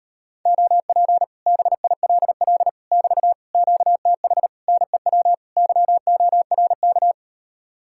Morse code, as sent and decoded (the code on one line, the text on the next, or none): --- .--. / -... .. .-.. .-.. / -...- / --.- - .... / -. .-- / -.-- --- .-. -.-
OP BILL <BT> QTH NEW YORK